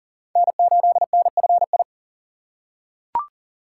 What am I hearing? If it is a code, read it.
Morse code, heard as a keyed tone, about 40 wpm, 700 Hz. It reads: N8NFI
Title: USA Call Signs - 3x Character Spacing - 40wpm